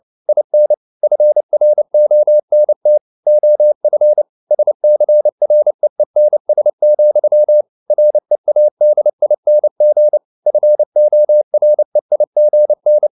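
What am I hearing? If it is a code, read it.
Morse code, heard as a keyed tone, about 29 wpm, 600 Hz. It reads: IN FRONT OF SCREENS, READING FOREIGN
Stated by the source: Morse code at 29 wpm, sending Ken Follett's novel Never in English, from CW med Gnister.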